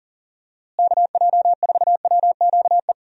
Morse code, heard as a keyed tone, about 40 wpm, 700 Hz. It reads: KJ4WQE
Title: USA Call Signs - 6 Characters - Repeated 2x - 40wpm